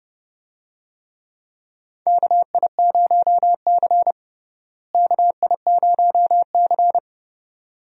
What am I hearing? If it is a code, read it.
Morse code, heard as a keyed tone, about 31 wpm, 700 Hz. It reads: KI0C KI0C
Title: USA Call Signs - 4 Characters - Repeated 2x - 30wpm